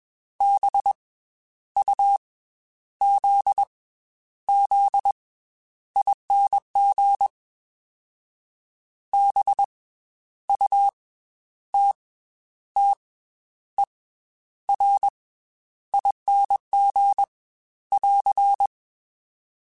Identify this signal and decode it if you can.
Morse code, heard as a keyed tone, about 21 wpm, 800 Hz. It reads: B U Z Z ING B U T T E R ING <AR>
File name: Special INGs / Special INGs-mp3s / Slow ing @20